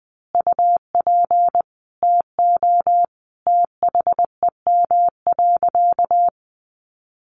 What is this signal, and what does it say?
Morse code, heard as a keyed tone, about 20 wpm, 700 Hz. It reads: UP TO THEM.
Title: Sentences from Top 1200 Words 20wpm